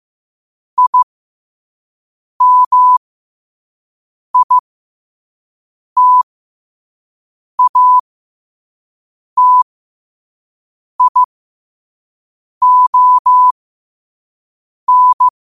Morse code, heard as a keyed tone, about 15 wpm, spaced as for 5 wpm, 1000 Hz. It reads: IMITATION
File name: qotdfiles / 5wpm